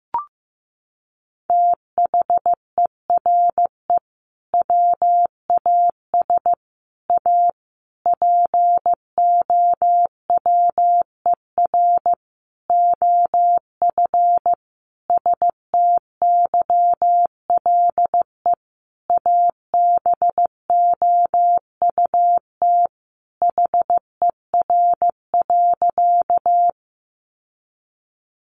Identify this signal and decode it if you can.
Morse code, heard as a keyed tone, about 15 wpm, 700 Hz. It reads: THERE WAS A POWER OF STYLE ABOUT HER.
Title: Sentences from Top 800 Words 15wpm